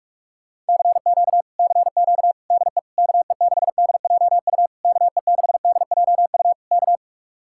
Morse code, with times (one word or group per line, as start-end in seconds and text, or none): 0.7-1.4 CQ
1.6-2.3 CQ
2.5-2.8 DE
3.0-4.7 KE6DJU
4.8-6.5 KE6DJU
6.7-7.0 K